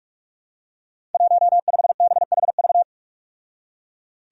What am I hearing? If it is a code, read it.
Morse code, heard as a keyed tone, about 44 wpm, 700 Hz. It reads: 15BHV
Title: Five Challenging Letter-Number Combinations 45wpm